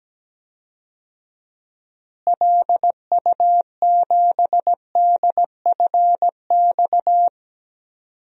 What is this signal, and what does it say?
Morse code, heard as a keyed tone, about 17 wpm, 700 Hz. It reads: LU7DFX
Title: DXCC-100 Call Signs - 6 Characters - Repeated 2x - 17wpm